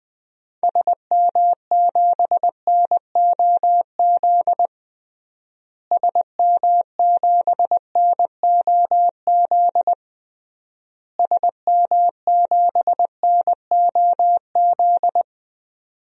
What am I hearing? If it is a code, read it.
Morse code, heard as a keyed tone, about 20 wpm, 700 Hz. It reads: SM7NOZ SM7NOZ SM7NOZ
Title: DXCC-100 Call Signs - 6 Characters - Repeated 3x - 20wpm